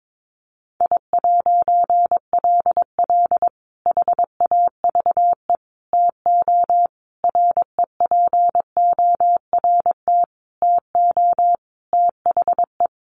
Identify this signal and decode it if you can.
Morse code, heard as a keyed tone, about 22 wpm, 700 Hz. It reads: I'LL HAVE TO REPORT TO THE
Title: A Sound Of Thunder 22wpm